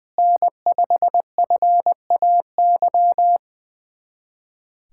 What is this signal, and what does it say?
Morse code, heard as a keyed tone, about 20 wpm, 700 Hz. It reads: N5FAY
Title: USA Callsign DE Callsign - 5 Characters - 20wpm